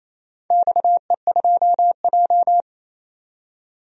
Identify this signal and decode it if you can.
Morse code, heard as a keyed tone, about 28 wpm, 700 Hz. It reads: XE2J
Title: DXCC-100 Call Signs - 4 Characters - 28wpm